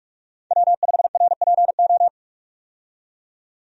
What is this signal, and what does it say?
Morse code, heard as a keyed tone, about 46 wpm, 700 Hz. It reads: W5RPO
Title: USA Call Signs - 5 Characters - 45wpm